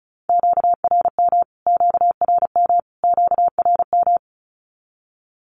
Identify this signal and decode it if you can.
Morse code, heard as a keyed tone, about 34 wpm, 700 Hz. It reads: QRM QRM QRM
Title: Q-Codes Repeated 3x 35wpm